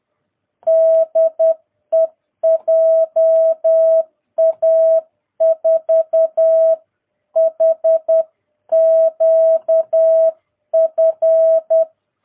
Morse code, heard as a keyed tone, about 10 wpm, 660 Hz. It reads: DE JA4HQF
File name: jh8cbh / 2022cw